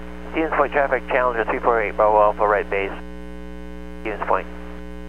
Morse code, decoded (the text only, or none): none